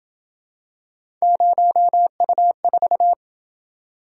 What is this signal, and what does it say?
Morse code, heard as a keyed tone, about 28 wpm, 700 Hz. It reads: 0U4